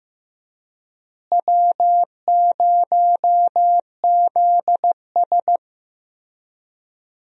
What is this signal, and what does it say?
Morse code, heard as a keyed tone, about 16 wpm, 700 Hz. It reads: W0ZS
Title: USA Call Signs - 4 Characters - Repeated 3x - 15wpm